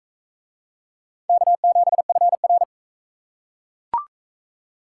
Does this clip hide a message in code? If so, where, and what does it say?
Yes: Morse code from 1.3 to 2.6 s, heard as a keyed tone, about 38 wpm, 700 Hz. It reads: K7FR